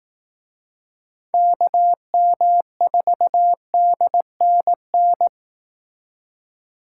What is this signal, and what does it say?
Morse code, heard as a keyed tone, about 18 wpm, 700 Hz. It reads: KM4DNN
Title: Speed Racing - USA Call Signs 15wpm